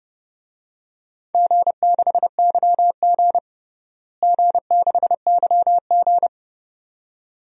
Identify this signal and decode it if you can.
Morse code, heard as a keyed tone, about 32 wpm, 700 Hz. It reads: G6YG G6YG
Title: DXCC-100 Call Signs - 4 Characters - Repeated 2x - 30wpm